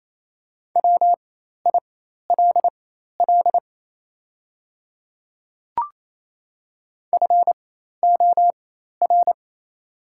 Morse code, heard as a keyed tone, about 28 wpm, spaced as for 13 wpm, 700 Hz. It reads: WILL FOR
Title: 4x Character Spacing - Top 200 Words - 28wpm